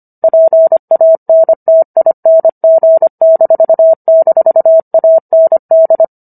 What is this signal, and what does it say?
Morse code, heard as a keyed tone, about 25 wpm, 650 Hz. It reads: PANTING--AND